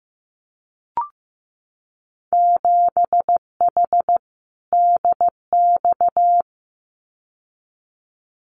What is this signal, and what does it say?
Morse code, heard as a keyed tone, about 15 wpm, 700 Hz. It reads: 7H DX
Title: Winter Field Day Exchanges Encore 15wpm